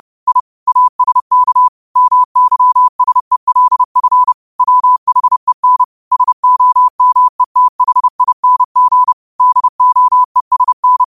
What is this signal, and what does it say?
Morse code, heard as a keyed tone, about 30 wpm, 1000 Hz. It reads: I ASK MYSELF WHEN SOMETHING DOESN
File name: qotdfiles / 30wpm